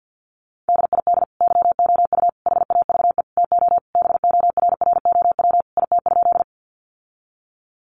Morse code, heard as a keyed tone, about 50 wpm, 700 Hz. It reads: DID YOU HAVE TO BORROW IT?
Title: Sentences from Top 1500 Words 50wpm